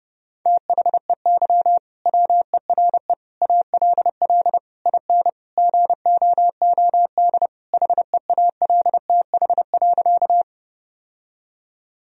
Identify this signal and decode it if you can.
Morse code, heard as a keyed tone, about 32 wpm, 700 Hz. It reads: THEY WERE ALL IN GOOD HEALTH.
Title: Sentences from Top 1000 Words - Review Part 3 - 30wpm